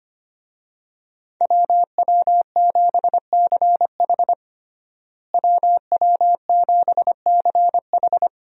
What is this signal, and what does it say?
Morse code, heard as a keyed tone, about 25 wpm, 700 Hz. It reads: WW7CH WW7CH
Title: USA Call Signs - 5 Characters - Repeated 2x - 25wpm